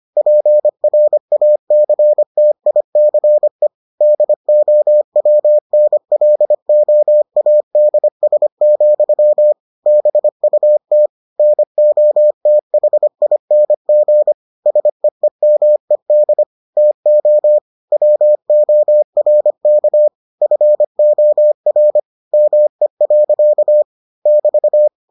Morse code, heard as a keyed tone, about 25 wpm, 600 Hz. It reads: PRACTICE DOWNLOADS, BUT NOTHING SEEMED TO WORK FOR ME. <BT>